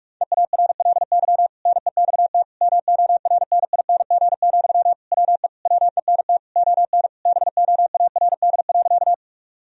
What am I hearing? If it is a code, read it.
Morse code, heard as a keyed tone, about 46 wpm, 700 Hz. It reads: EARLY NEXT MORNING, WE WENT ON BOARD.
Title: Sentences from Top 1000 Words - Review Part 4 - 45wpm